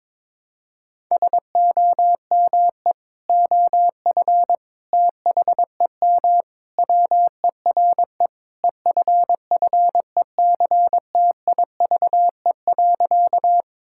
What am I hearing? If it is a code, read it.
Morse code, heard as a keyed tone, about 22 wpm, 700 Hz. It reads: SOME OF THEM WERE EFFECTIVE.
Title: Sentences from Top 1000 Words - Review Part 6 - 22wpm